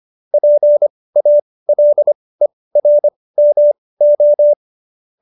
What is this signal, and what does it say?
Morse code, heard as a keyed tone, about 25 wpm, spaced as for 18 wpm, 590 Hz. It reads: PALERMO
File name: qso215b 18